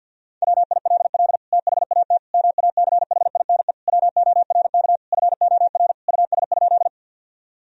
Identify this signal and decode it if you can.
Morse code, heard as a keyed tone, about 50 wpm, 700 Hz. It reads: WILL THAT MACHINE WORK FOR US?
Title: Sentences from Top 1000 Words - Review Part 2 - 50wpm